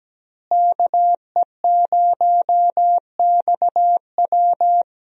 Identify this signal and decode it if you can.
Morse code, heard as a keyed tone, about 17 wpm, 700 Hz. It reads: KE0XW
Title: USA Call Signs - 5 Characters - Repeated 2x - 17wpm